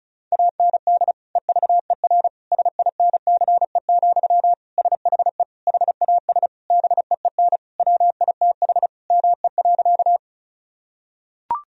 Morse code, heard as a keyed tone, about 36 wpm, 700 Hz. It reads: AND EVER SINCE, SHE HAS BEEN WITH ME.
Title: Sentences from Top 1000 Words - Review Part 1 - 35wpm